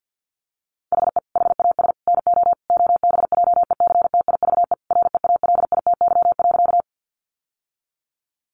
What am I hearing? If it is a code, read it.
Morse code, heard as a keyed tone, about 50 wpm, 700 Hz. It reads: HE HAS NO OBJECTIVE REALITY.
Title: Sentences from Top 1200 Words 50wpm